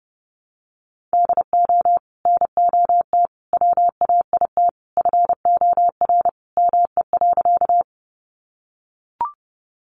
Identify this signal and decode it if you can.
Morse code, heard as a keyed tone, about 30 wpm, 700 Hz. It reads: DO NOT WAIT FOR ME.
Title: Sentences from Top 600 Words 30wpm